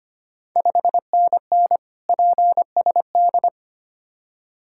Morse code, heard as a keyed tone, about 25 wpm, 700 Hz. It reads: 5NN PSD
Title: Ontario QSO Party Exchange 25wpm